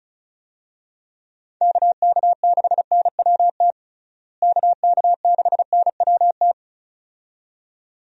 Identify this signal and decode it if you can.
Morse code, heard as a keyed tone, about 34 wpm, 700 Hz. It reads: KK6NWT KK6NWT
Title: USA Call Signs - 6 Characters - Repeated 2x - 35wpm